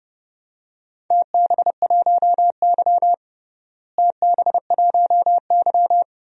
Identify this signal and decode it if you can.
Morse code, heard as a keyed tone, about 32 wpm, 700 Hz. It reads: TB1Y TB1Y
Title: DXCC-100 Call Signs - 4 Characters - Repeated 3x - 30wpm